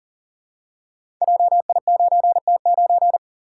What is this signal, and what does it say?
Morse code, heard as a keyed tone, about 39 wpm, 700 Hz. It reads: JI9T9
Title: Code Groups - 5 Letter:Numbers - 40wpm